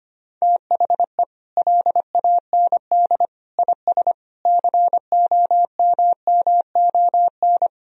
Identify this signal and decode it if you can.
Morse code, heard as a keyed tone, about 25 wpm, 700 Hz. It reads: THE LAND IS COMMON
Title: Sentences from Top 1000 Words - Review Part 3 - 25wpm